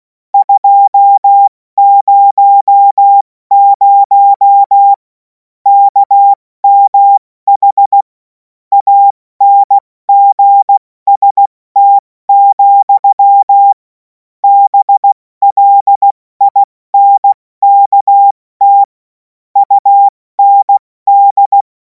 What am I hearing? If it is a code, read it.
Morse code, heard as a keyed tone, about 16 wpm, 800 Hz. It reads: 200 KMH ANGST, BLINKT UND
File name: Morsekurs / Witze / Witz 05. Mai 2026 70 F80